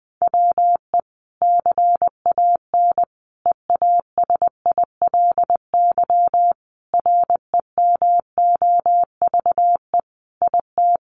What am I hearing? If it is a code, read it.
Morse code, heard as a keyed tone, about 20 wpm, 700 Hz. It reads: WE CAN EASILY REMOVE IT
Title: Sentences from Top 1200 Words 20wpm